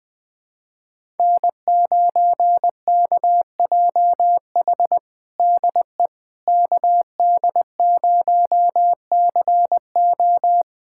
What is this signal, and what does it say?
Morse code, heard as a keyed tone, about 20 wpm, 700 Hz. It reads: N9KJH DE KD0CO
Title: USA Callsign DE Callsign - 5 Characters - 20wpm